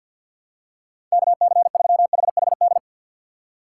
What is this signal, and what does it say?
Morse code, heard as a keyed tone, about 53 wpm, 700 Hz. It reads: KC3HHD
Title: USA Call Signs - 6 Characters - 6x Character Spacing - 50wpm